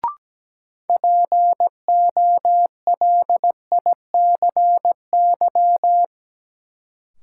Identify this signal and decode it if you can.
Morse code, heard as a keyed tone, about 17 wpm, 700 Hz. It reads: POLICY
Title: Sentences from Top 1000 Words - Review Part 5 - New Word Drill - 17wpm